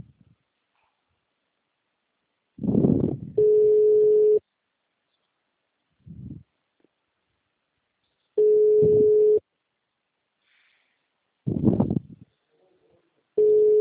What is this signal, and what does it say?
Ringback tone (Europe, 425 Hz).
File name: out- -2006- - - .80788